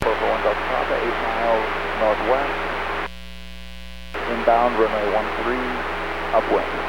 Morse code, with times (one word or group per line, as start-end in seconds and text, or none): none